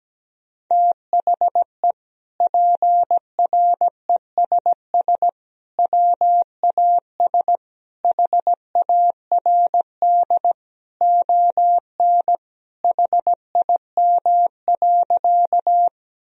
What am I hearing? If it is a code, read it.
Morse code, heard as a keyed tone, about 17 wpm, 700 Hz. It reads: THE PRESS WAS HARD ON HIM.
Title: Sentences from Top 1000 Words - Review Part 1 - 17wpm